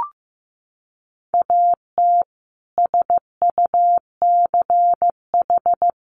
Morse code, heard as a keyed tone, about 15 wpm, 700 Hz. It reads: AT SUCH
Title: Sentences from Top 1000 Words 15wpm